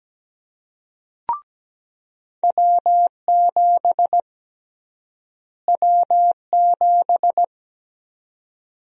Morse code, heard as a keyed tone, about 17 wpm, 700 Hz. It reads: W7 W7
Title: USA Call Sign Prefixes - Repeated 2x - 17wpm